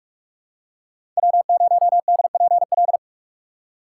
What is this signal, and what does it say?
Morse code, heard as a keyed tone, about 43 wpm, 700 Hz. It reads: W0DPL
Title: USA Call Signs - 5 Characters - 4x Character Spacing - 45wpm